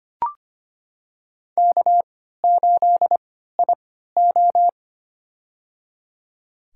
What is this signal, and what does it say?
Morse code, heard as a keyed tone, about 25 wpm, spaced as for 14 wpm, 700 Hz. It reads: K8IO
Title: USA Call Signs - 4 Characters - 3x Character Spacing - 25wpm